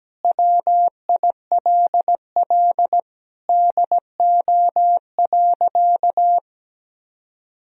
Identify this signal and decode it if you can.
Morse code, heard as a keyed tone, about 17 wpm, 700 Hz. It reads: WILL DO.
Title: Sentences from Top 1000 Words - Review Part 2 - 17wpm